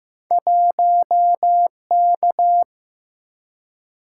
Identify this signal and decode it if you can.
Morse code, heard as a keyed tone, about 15 wpm, 700 Hz. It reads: 1K
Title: Single-Letter Single-Number 15wpm